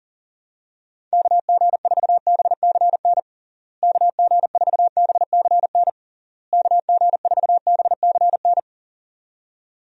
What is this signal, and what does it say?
Morse code, heard as a keyed tone, about 40 wpm, 700 Hz. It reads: KG4BCN KG4BCN KG4BCN
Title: USA Call Signs - 6 Characters - Repeated 3x - 40wpm